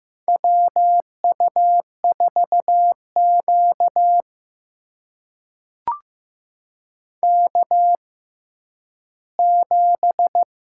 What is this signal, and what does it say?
Morse code, heard as a keyed tone, about 15 wpm, spaced as for 10 wpm, 700 Hz. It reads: WU4Q K7
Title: USA Call Signs - 4 Characters - 6x Character Spacing - 15wpm